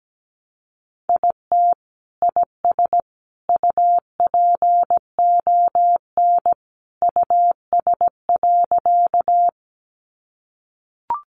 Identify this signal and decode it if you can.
Morse code, heard as a keyed tone, about 17 wpm, 700 Hz. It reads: IT IS UPON US.
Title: Sentences from Top 1100 Words 17wpm